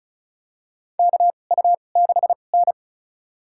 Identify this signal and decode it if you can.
Morse code, heard as a keyed tone, about 34 wpm, spaced as for 25 wpm, 700 Hz. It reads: KU6N
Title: USA Call Signs - 4 Characters - 2x Character Spacing - 35wpm